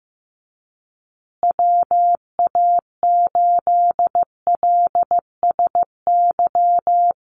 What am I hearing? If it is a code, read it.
Morse code, heard as a keyed tone, about 15 wpm, 700 Hz. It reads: WA8LSY
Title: USA Call Signs Repeated 3x 15wpm